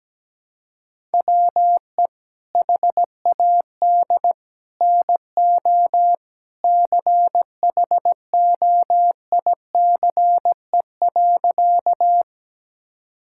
Morse code, heard as a keyed tone, about 17 wpm, 700 Hz. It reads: WE HAD NO CHOICE.
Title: Sentences from Top 1000 Words - Review Part 6 - 17wpm